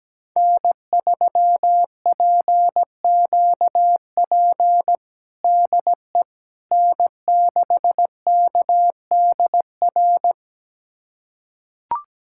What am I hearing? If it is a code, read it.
Morse code, heard as a keyed tone, about 17 wpm, 700 Hz. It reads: N3PQP DE N6KDR
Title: USA Callsign DE Callsign - 5 Characters - 17wpm